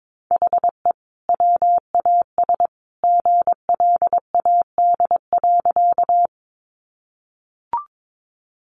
Morse code, heard as a keyed tone, about 22 wpm, 700 Hz. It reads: HE WAS GLAD.
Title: Sentences from Top 800 Words 22wpm